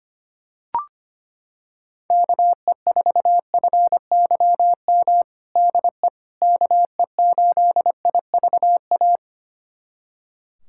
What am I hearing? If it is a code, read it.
Morse code, heard as a keyed tone, about 25 wpm, 700 Hz. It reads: KE4FYM DE KE8IVA
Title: USA Callsign DE Callsign - 6 Characters - 25wpm